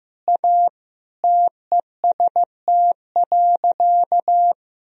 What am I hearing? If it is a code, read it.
Morse code, heard as a keyed tone, about 15 wpm, 700 Hz. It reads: A TEST.
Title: Sentences from Top 1000 Words - Review Part 3 - 15wpm